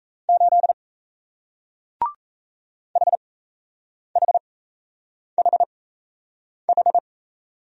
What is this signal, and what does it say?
Morse code, heard as a keyed tone, about 37 wpm, spaced as for 8 wpm, 700 Hz. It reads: 8 HHHH